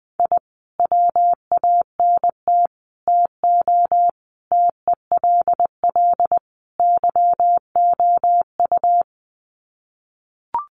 Morse code, heard as a keyed tone, about 20 wpm, 700 Hz. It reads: I WANT TO TELL YOU
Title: Sets of 5 Words Encore 20wpm